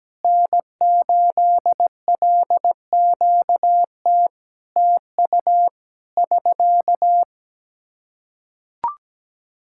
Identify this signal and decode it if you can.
Morse code, heard as a keyed tone, about 17 wpm, 700 Hz. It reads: N8LQT TU <SK>
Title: Practice QSO - Exchange 4 - Ending - 17wpm